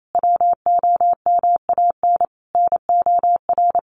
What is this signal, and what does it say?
Morse code, heard as a keyed tone, about 28 wpm, 700 Hz. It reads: WOMAN NOR
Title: Sentences from Top 1500 Words 28wpm